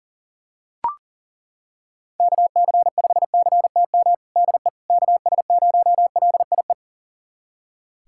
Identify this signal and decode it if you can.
Morse code, heard as a keyed tone, about 40 wpm, 700 Hz. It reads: KC5CTM DE KS0LIE
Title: USA Callsign DE Callsign - 6 Characters - 40wpm